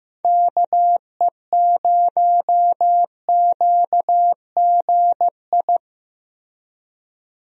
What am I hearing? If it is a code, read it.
Morse code, heard as a keyed tone, about 15 wpm, 700 Hz. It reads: KE0QGI
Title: USA Call Signs - 6 Characters - Repeated 2x - 15wpm